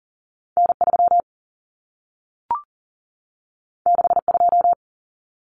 Morse code, heard as a keyed tone, about 40 wpm, 700 Hz. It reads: N3 62
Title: Common Cut Numbers - 2-Digit 40wpm